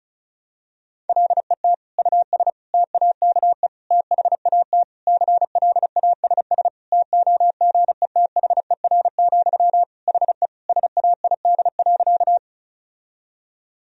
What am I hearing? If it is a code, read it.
Morse code, heard as a keyed tone, about 34 wpm, 700 Hz. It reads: LET US TAKE THAT CLASS TOGETHER, HE SAID.
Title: Sentences from Top 1000 Words - Review Part 2 - 35wpm